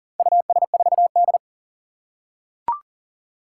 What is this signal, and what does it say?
Morse code, heard as a keyed tone, about 40 wpm, 700 Hz. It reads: US4D
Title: Four Challenging Letter-Number Combinations 40wpm